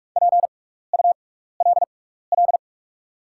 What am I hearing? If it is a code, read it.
Morse code, heard as a keyed tone, about 48 wpm, spaced as for 16 wpm, 690 Hz. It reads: PULL
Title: 6x Character Spacing - Top 500 Words - 45wpm